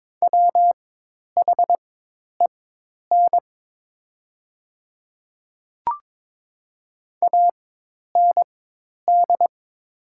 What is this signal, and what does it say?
Morse code, heard as a keyed tone, about 22 wpm, spaced as for 10 wpm, 700 Hz. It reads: WHEN AND
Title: 4x Character Spacing - Top 100 Words - 22wpm